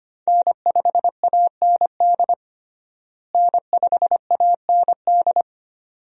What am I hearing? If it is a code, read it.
Morse code, heard as a keyed tone, about 25 wpm, 700 Hz. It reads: N5AND N5AND
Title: USA Call Signs - 5 Characters - Repeated 2x - 25wpm